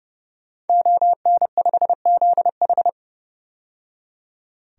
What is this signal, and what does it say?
Morse code, heard as a keyed tone, about 30 wpm, 700 Hz. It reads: ON5ZH
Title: DXCC-100 Call Signs - 5 Characters - 30wpm